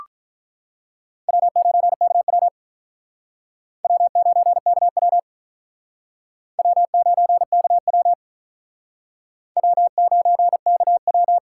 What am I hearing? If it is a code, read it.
Morse code, heard as a keyed tone, about 48 wpm, 700 Hz. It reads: W9KW W9KW W9KW W9KW